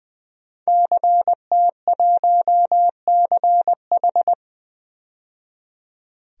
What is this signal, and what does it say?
Morse code, heard as a keyed tone, about 21 wpm, 700 Hz. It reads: CT1CH
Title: DXCC-100 Call Signs - 5 Characters - 20wpm